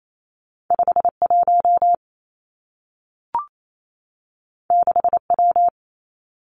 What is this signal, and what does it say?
Morse code, heard as a keyed tone, about 28 wpm, 700 Hz. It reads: 5 1 6 W
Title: Two Challenging Letter-Number Combinations 28wpm